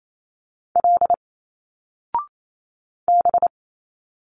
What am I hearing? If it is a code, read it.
Morse code, heard as a keyed tone, about 28 wpm, 700 Hz. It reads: L B
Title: Single Letters 28wpm